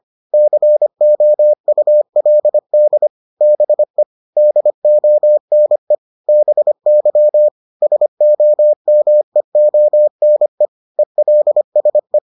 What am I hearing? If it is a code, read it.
Morse code, heard as a keyed tone, about 25 wpm, 600 Hz. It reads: COULD BE DONE BY SOMEONE ELSE